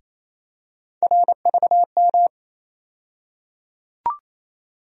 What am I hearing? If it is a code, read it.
Morse code, heard as a keyed tone, about 28 wpm, 700 Hz. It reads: RVM
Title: USA Call Sign Postfixes - 3 Letters 28wpm